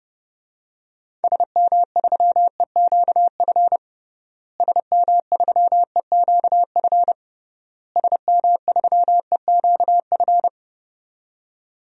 Morse code, heard as a keyed tone, about 30 wpm, 700 Hz. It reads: SM3EQF SM3EQF SM3EQF
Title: DXCC-100 Call Signs - 6 Characters - Repeated 3x - 30wpm